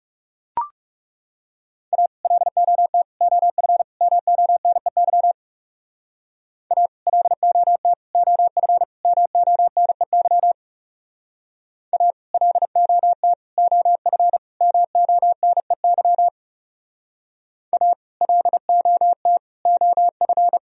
Morse code, heard as a keyed tone, about 38 wpm, 700 Hz. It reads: A LOT OF MONEY A LOT OF MONEY A LOT OF MONEY A LOT OF